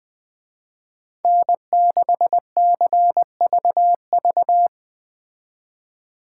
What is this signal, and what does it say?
Morse code, heard as a keyed tone, about 20 wpm, 700 Hz. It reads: N6CVV
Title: Speed Racing - USA Call Signs 17wpm